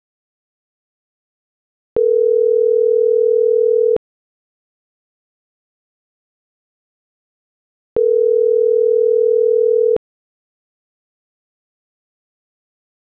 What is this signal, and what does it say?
Ringback tone (North America), 440 + 480 Hz.